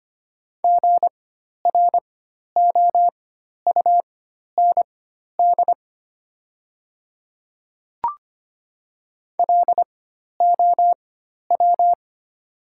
Morse code, heard as a keyed tone, about 25 wpm, spaced as for 12 wpm, 700 Hz. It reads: GROUND LOW
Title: 4x Character Spacing - Top 500 Words - 25wpm